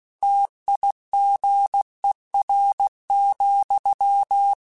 Morse code, heard as a keyed tone, about 16 wpm, 800 Hz. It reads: TIGER,
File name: Misc / Misc-mp3s / Word Pairs @15